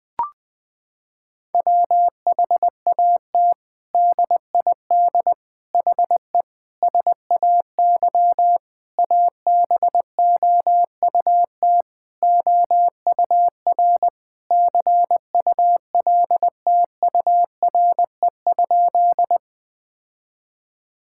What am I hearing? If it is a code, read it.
Morse code, heard as a keyed tone, about 20 wpm, 700 Hz. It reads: WHAT DID HE SAY ABOUT OUR CULTURE?
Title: Sentences from Top 1000 Words - Review Part 6 - 20wpm